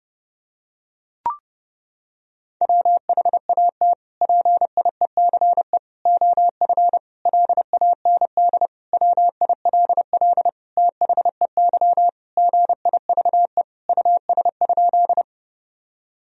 Morse code, heard as a keyed tone, about 30 wpm, 700 Hz. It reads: WHAT PIECE OF LAND WILL THEY GIVE US?
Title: Sentences from Top 1000 Words - Review Part 2 - 30wpm